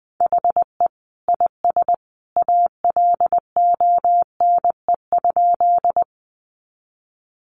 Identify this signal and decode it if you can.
Morse code, heard as a keyed tone, about 19 wpm, 700 Hz. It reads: HE IS ALONE?
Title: Sentences from Top 1000 Words 20wpm